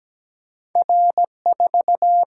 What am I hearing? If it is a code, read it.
Morse code, heard as a keyed tone, about 17 wpm, 700 Hz. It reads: R4
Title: Single-Letter Single-Number 17wpm